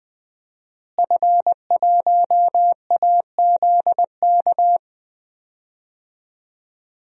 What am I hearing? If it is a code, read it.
Morse code, heard as a keyed tone, about 20 wpm, 700 Hz. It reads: F1AZK
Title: DXCC-100 Call Signs - 5 Characters - Repeated 2x - 20wpm